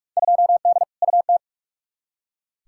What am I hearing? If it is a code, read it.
Morse code, heard as a keyed tone, about 43 wpm, 700 Hz. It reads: 2D UT
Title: Field Day Exchanges Encore - 45wpm